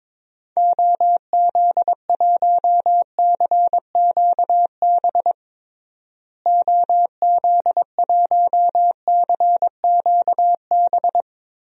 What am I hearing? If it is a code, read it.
Morse code, heard as a keyed tone, about 21 wpm, 700 Hz. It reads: OZ1CQB OZ1CQB
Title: DXCC-100 Call Signs - 6 Characters - Repeated 3x - 22wpm